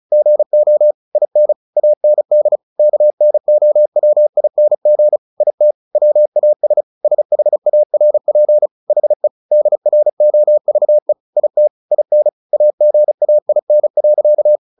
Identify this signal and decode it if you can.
Morse code, heard as a keyed tone, about 34 wpm, 600 Hz. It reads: GO IN AND KNOWING IT WAS SHARP HE DROVE IT IN AGAIN.
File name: the old man and the sea 2 35wpm